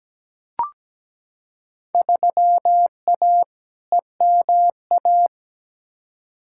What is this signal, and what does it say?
Morse code, heard as a keyed tone, about 17 wpm, 700 Hz. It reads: 3A EMA
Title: Field Day Exchanges - 17wpm